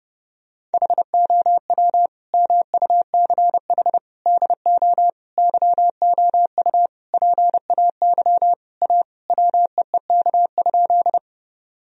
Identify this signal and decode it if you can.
Morse code, heard as a keyed tone, about 28 wpm, 700 Hz. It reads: HOW MUCH DO YOU PAY A WEEK?
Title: Sentences from Top 1000 Words - Review Part 2 - 30wpm